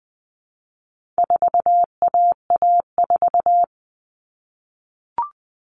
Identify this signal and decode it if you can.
Morse code, heard as a keyed tone, about 20 wpm, 700 Hz. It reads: 4AA4
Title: All Cut Numbers - 4-Digit 20wpm